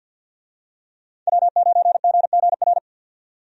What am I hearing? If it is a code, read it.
Morse code, heard as a keyed tone, about 46 wpm, 700 Hz. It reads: W9GGR
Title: USA Call Signs - 5 Characters - 50wpm